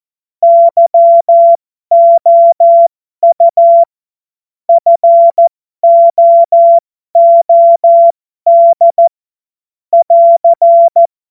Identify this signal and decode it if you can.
Morse code, heard as a keyed tone, about 14 wpm, spaced as for 12 wpm, 680 Hz. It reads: YOU FOOD <AR>
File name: Morsekurs / Witze